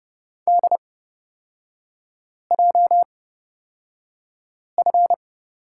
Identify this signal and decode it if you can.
Morse code, heard as a keyed tone, about 30 wpm, spaced as for 5 wpm, 700 Hz. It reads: DJF
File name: Lesson 37e - Letter x - Three Characters 30wpm